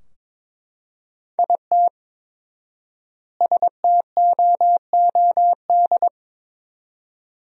Morse code, heard as a keyed tone, about 22 wpm, 700 Hz. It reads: IT STOOD
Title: 4x Word-Spacing - Sentences from Top 500 Words - 22wpm